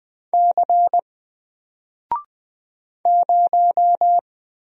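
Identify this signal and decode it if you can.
Morse code, heard as a keyed tone, about 20 wpm, 700 Hz. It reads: C 0
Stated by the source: Single letter and number Morse drill at 20 wpm, from N4LCD.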